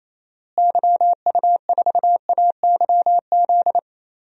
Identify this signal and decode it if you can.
Morse code, heard as a keyed tone, about 28 wpm, 700 Hz. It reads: YU4AYZ
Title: DXCC-100 Call Signs - 6 Characters - Repeated 3x - 28wpm